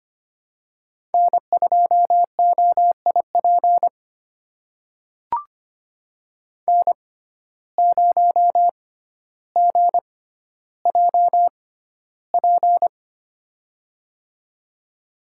Morse code, heard as a keyed tone, about 25 wpm, spaced as for 13 wpm, 700 Hz. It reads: N2OIP N0GJP
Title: USA Call Signs - 5 Characters - 6x Character Spacing - 25wpm